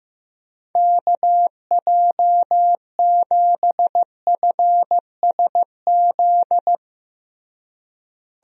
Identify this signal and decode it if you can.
Morse code, heard as a keyed tone, about 15 wpm, 700 Hz. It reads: KJ7FSZ
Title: USA Call Signs - 6 Characters - Repeated 2x - 15wpm